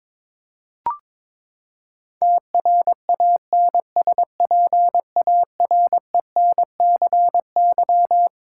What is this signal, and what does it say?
Morse code, heard as a keyed tone, about 22 wpm, 700 Hz. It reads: TRANSPARENCY
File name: Twelve Letter Words - 22wpm